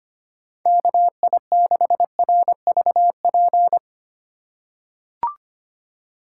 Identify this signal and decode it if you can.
Morse code, heard as a keyed tone, about 25 wpm, 700 Hz. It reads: KI6RVP
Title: USA Call Signs - 6 Characters - 3x Character Spacing - 25wpm